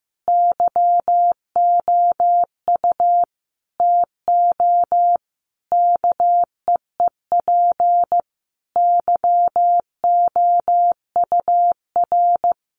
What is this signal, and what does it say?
Morse code, heard as a keyed tone, about 15 wpm, 700 Hz. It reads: YOU TO KEEP YOUR
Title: Sentences from Top 1100 Words 15wpm